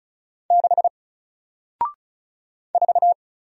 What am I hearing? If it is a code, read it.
Morse code, heard as a keyed tone, about 36 wpm, 700 Hz. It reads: 6 4
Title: Single Character (Letter:Number:Punctuation:Prosign) Alt 35wpm